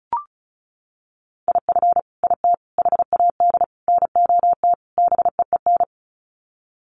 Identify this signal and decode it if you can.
Morse code, heard as a keyed tone, about 34 wpm, 700 Hz. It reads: IF IT HAD NOT BEEN
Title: Sets of 5 Words Encore 35wpm